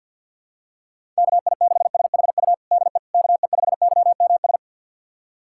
Morse code, heard as a keyed tone, about 51 wpm, 700 Hz. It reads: KI6SHU DE KE5YMS
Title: USA Callsign DE Callsign - 6 Characters - 50wpm